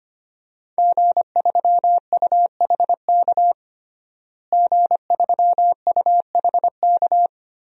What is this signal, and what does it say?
Morse code, heard as a keyed tone, about 25 wpm, 700 Hz. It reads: G3UHK G3UHK
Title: DXCC-100 Call Signs - 5 Characters - Repeated 3x - 25wpm